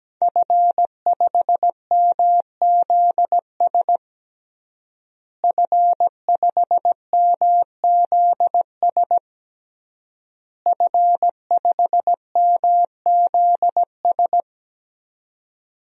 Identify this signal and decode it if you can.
Morse code, heard as a keyed tone, about 17 wpm, 700 Hz. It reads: F5MZS F5MZS F5MZS
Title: DXCC-100 Call Signs - 5 Characters - Repeated 3x - 17wpm